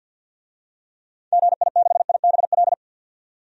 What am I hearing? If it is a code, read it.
Morse code, heard as a keyed tone, about 51 wpm, 700 Hz. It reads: GI6IBL
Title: DXCC-100 Call Signs - 6 Characters - 50wpm